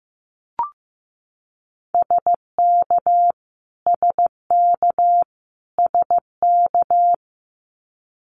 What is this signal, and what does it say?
Morse code, heard as a keyed tone, about 15 wpm, 700 Hz. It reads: SK SK SK
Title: Canadian Provinces and Territories - Repeated 3x - 15wpm